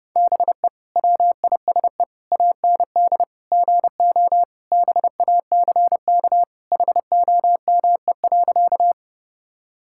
Morse code, heard as a keyed tone, about 32 wpm, 700 Hz. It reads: BE WISE AND GO BACK HOME.
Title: Sentences from Top 1600 Words 30wpm